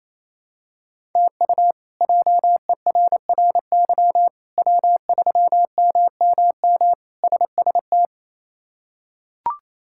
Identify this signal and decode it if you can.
Morse code, heard as a keyed tone, about 28 wpm, 700 Hz. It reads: TU JERRY W3MMM SST
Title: K1USN SST QSO Practice - 28wpm